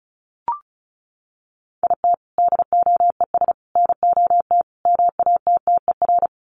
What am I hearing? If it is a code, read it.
Morse code, heard as a keyed tone, about 34 wpm, 700 Hz. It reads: IT DOES NOT MATTER
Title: Sets of 4 Words 35wpm